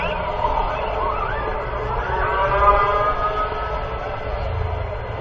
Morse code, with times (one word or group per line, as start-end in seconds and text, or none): none